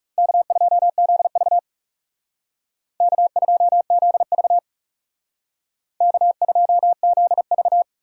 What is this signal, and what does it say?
Morse code, heard as a keyed tone, about 35 wpm, 700 Hz. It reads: K2ZV K2ZV K2ZV
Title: Speed Racing - USA 1x2 Call Signs 30wpm